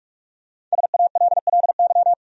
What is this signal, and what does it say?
Morse code, heard as a keyed tone, about 46 wpm, 700 Hz. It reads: SALLY
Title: USA Common Names - 45wpm